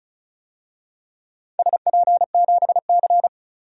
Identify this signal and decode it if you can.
Morse code, heard as a keyed tone, about 36 wpm, 700 Hz. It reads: SP7C
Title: DXCC-100 Call Signs - 4 Characters - 35wpm